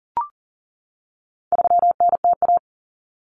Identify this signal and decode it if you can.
Morse code, heard as a keyed tone, about 40 wpm, 700 Hz. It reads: 3NTA
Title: Common Cut Numbers - 4-Digit 40wpm